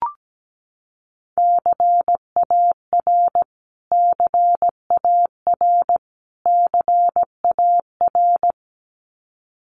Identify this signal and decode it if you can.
Morse code, heard as a keyed tone, about 17 wpm, 700 Hz. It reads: CAR CAR CAR
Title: Top 500 Words Repeated 17wpm